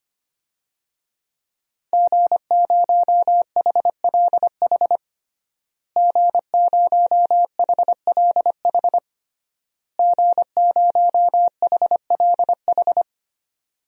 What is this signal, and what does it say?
Morse code, heard as a keyed tone, about 25 wpm, 700 Hz. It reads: G0HLH G0HLH G0HLH
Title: DXCC-100 Call Signs - 5 Characters - Repeated 3x - 25wpm